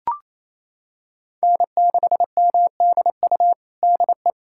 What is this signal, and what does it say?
Morse code, heard as a keyed tone, about 28 wpm, 700 Hz. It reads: N6MDU DE
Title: USA Callsign DE Callsign - 5 Characters - 28wpm